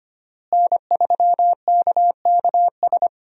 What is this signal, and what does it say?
Morse code, heard as a keyed tone, about 25 wpm, 700 Hz. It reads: N3KKS
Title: USA Call Signs - 5 Characters - Repeated 3x - 25wpm